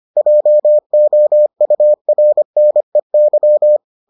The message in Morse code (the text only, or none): JOURNEY